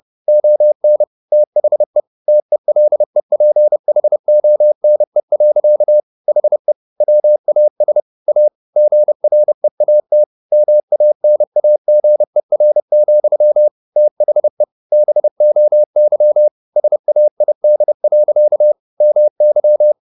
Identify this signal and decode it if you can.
Morse code, heard as a keyed tone, about 30 wpm, 600 Hz. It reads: ON THE TELEPHONE. HE WAS A GREAT MANAGER, THE BOY SAID. MY
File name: the old man and the sea / the old man and the sea 1 30wpm